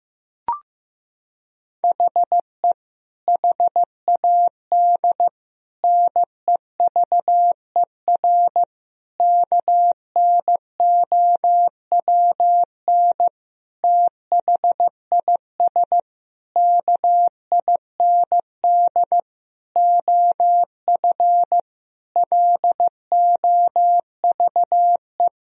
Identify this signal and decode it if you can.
Morse code, heard as a keyed tone, about 15 wpm, 700 Hz. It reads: HE HAD NEVER KNOWN THIS KIND OF LOVE
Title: Sentences from Top 1000 Words - Review Part 3 - 15wpm